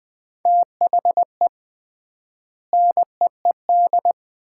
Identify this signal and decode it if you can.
Morse code, heard as a keyed tone, about 20 wpm, 700 Hz. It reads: THE NEED
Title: 3x Word-Spacing - Sets of 3 Words - 20wpm